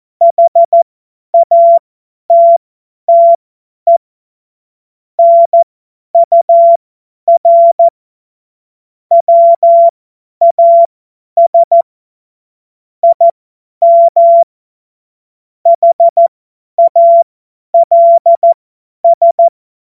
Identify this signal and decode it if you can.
Morse code, heard as a keyed tone, about 14 wpm, spaced as for 10 wpm, 680 Hz. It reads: HATTE NUR WAS IM HALS
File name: Morsekurs / Witze / Witz 30. November 2021, 50 F70